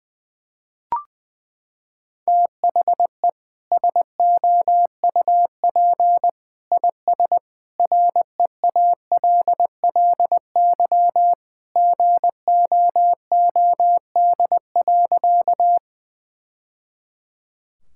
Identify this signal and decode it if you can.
Morse code, heard as a keyed tone, about 21 wpm, 700 Hz. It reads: THE SOUP IS REALLY GOOD.
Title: Sentences from Top 1000 Words - Review Part 6 - 20wpm